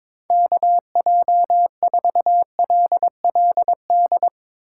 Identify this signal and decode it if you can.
Morse code, heard as a keyed tone, about 24 wpm, 700 Hz. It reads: KJ4LLD
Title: Speed Racing - USA Call Signs 22wpm